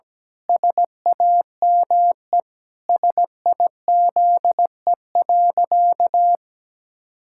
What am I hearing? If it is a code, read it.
Morse code, heard as a keyed tone, about 17 wpm, 700 Hz. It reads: SAME SIZE.
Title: Pangram Challenge v2 - 17wpm